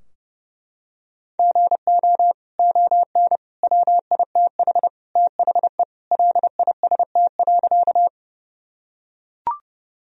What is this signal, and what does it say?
Morse code, heard as a keyed tone, about 30 wpm, 700 Hz. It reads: GO ON WITH THE LIST.
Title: Sentences from Top 1000 Words - Review Part 2 - 30wpm